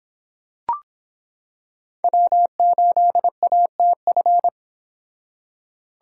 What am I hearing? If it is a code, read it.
Morse code, heard as a keyed tone, about 26 wpm, 700 Hz. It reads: W8ATF